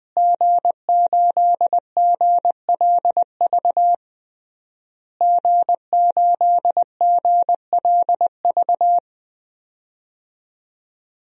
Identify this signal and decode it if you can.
Morse code, heard as a keyed tone, about 20 wpm, 700 Hz. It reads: G8GLV G8GLV
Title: DXCC-100 Call Signs - 5 Characters - Repeated 3x - 20wpm